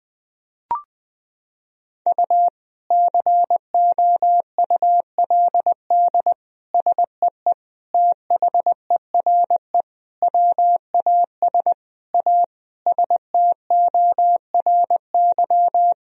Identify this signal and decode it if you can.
Morse code, heard as a keyed tone, about 20 wpm, 700 Hz. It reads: U COULD SEE THERE WAS A STORY
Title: Sentences from Top 500 Words and Abbreviations 20wpm